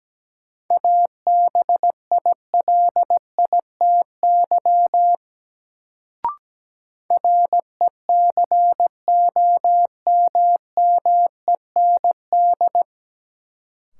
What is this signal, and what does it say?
Morse code, heard as a keyed tone, about 17 wpm, 700 Hz. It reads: ABILITY RECOMMEND
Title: Sentences from Top 1200 Words - New Word Drill Alt - 17wpm